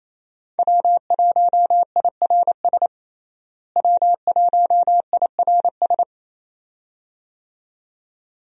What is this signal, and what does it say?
Morse code, heard as a keyed tone, about 27 wpm, 700 Hz. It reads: W1IRS W1IRS
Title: USA Call Signs - 5 Characters - Repeated 2x - 28wpm